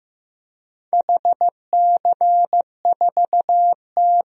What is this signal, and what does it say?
Morse code, heard as a keyed tone, about 15 wpm, 700 Hz. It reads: HC4T
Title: DXCC-100 Call Signs - 4 Characters - 15wpm